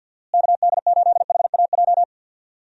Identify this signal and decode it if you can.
Morse code, heard as a keyed tone, about 48 wpm, 700 Hz. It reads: KD8HAJ
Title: USA Call Signs - 6 Characters - Repeated 3x - 50wpm